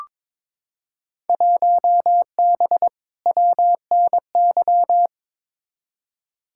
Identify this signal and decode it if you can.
Morse code, heard as a keyed tone, about 22 wpm, 700 Hz. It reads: 1B WNY
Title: Field Day Exchanges - 22wpm